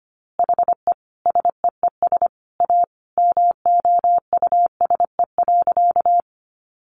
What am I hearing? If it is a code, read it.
Morse code, heard as a keyed tone, about 25 wpm, 700 Hz. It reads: HE SEES A MOUSE.
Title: Easy Phrases 25wpm